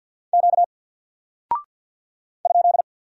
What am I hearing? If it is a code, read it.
Morse code, heard as a keyed tone, about 48 wpm, 700 Hz. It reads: Q ?
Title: Single Character (Letter:Number:Punctuation:Prosign) Alt 50wpm